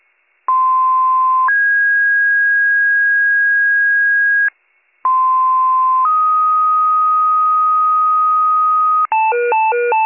Two-tone page, heard as a one-second tone, then a three-second tone, about 1040 then 1650 Hz, about 1040 then 1250 Hz, with noise. It is heard from 0.5 to 9.1 s.